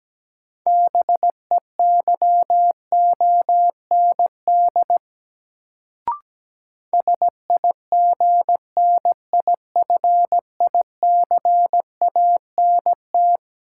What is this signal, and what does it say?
Morse code, heard as a keyed tone, about 17 wpm, 700 Hz. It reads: BEYOND SIGNIFICANT
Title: Sentences from Top 1000 Words - Review Part 6 - New Word Drill - 17wpm